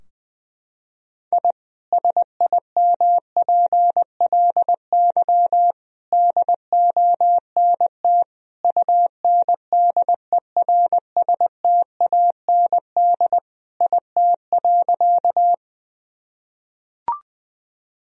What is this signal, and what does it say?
Morse code, heard as a keyed tone, about 21 wpm, 700 Hz. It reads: I SIMPLY DONT UNDERSTAND IT.
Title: Sentences from Top 1000 Words - Review Part 4 - 20wpm